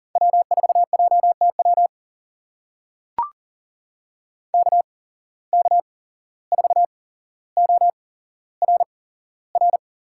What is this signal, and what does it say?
Morse code, heard as a keyed tone, about 40 wpm, spaced as for 16 wpm, 700 Hz. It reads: W4JTW KK4ORR
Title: USA Call Signs - 8x Character Spacing - 40wpm